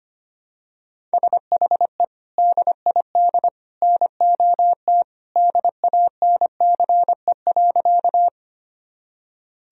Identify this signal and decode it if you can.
Morse code, heard as a keyed tone, about 25 wpm, 700 Hz. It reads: SHE DID NOT DANCE.
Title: Sentences from Top 1000 Words - Review Part 4 - 25wpm